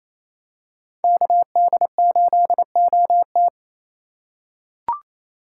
Morse code, heard as a keyed tone, about 28 wpm, 700 Hz. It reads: KD8OT
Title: USA Call Signs - 5 Characters - 2x Character Spacing - 28wpm